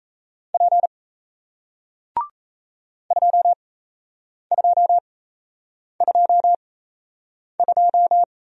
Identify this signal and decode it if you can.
Morse code, heard as a keyed tone, about 37 wpm, spaced as for 8 wpm, 700 Hz. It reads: P 2222